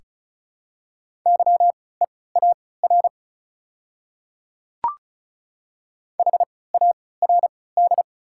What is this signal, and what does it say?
Morse code, heard as a keyed tone, about 34 wpm, spaced as for 20 wpm, 700 Hz. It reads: YEAR HARD
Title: 3x Character Spacing - Top 300 Words - 35wpm